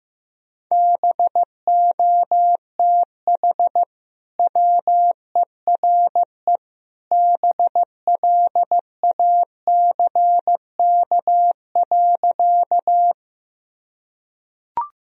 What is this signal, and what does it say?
Morse code, heard as a keyed tone, about 15 wpm, 700 Hz. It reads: BOTH WERE BLACK.
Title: Sentences from Top 1000 Words - Review Part 2 - 15wpm